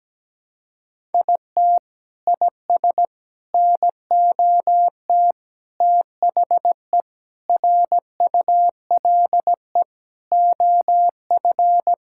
Morse code, heard as a keyed tone, about 17 wpm, 700 Hz. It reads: IT IS NOT THE RULE OF
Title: Sentences from Top 1000 Words - Review Part 5 - 17wpm